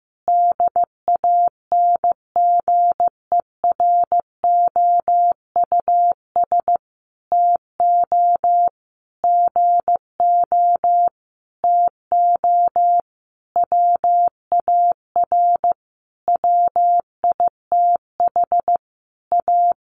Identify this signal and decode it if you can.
Morse code, heard as a keyed tone, about 15 wpm, 700 Hz. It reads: DANGEROUS TO GO TO WAR WITH A
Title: Sentences from Top 1200 Words 15wpm